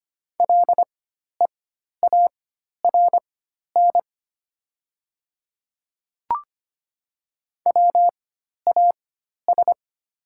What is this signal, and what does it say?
Morse code, heard as a keyed tone, about 25 wpm, spaced as for 12 wpm, 700 Hz. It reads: LEARN WAS
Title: 4x Character Spacing - Top 300 Words - 25wpm